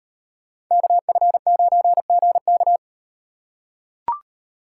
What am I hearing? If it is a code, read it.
Morse code, heard as a keyed tone, about 38 wpm, 700 Hz. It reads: KF9GK